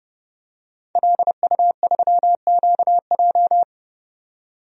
Morse code, heard as a keyed tone, about 30 wpm, 700 Hz. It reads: LU3QJ
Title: DXCC-100 Call Signs - 5 Characters - 30wpm